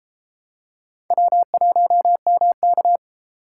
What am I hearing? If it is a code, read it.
Morse code, heard as a keyed tone, about 32 wpm, 700 Hz. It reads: W1MK